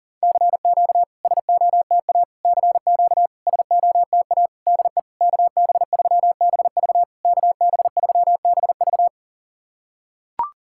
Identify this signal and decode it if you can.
Morse code, heard as a keyed tone, about 40 wpm, 700 Hz. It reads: CQ SOTA CQ SOTA DE KB3BV KB3BV
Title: SOTA QSO Practice 40wpm